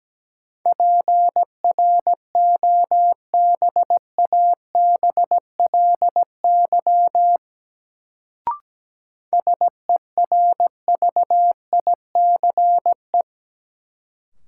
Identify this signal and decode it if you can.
Morse code, heard as a keyed tone, about 17 wpm, 700 Hz. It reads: PROBABLY SERVICE
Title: Sentences from Top 700 Words - New Word Drill - 17wpm